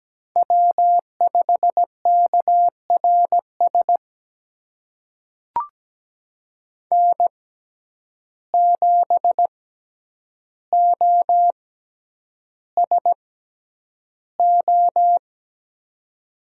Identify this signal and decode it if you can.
Morse code, heard as a keyed tone, about 17 wpm, spaced as for 9 wpm, 700 Hz. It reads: W5KRS N7OSO
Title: USA Call Signs - 5 Characters - 6x Character Spacing - 17wpm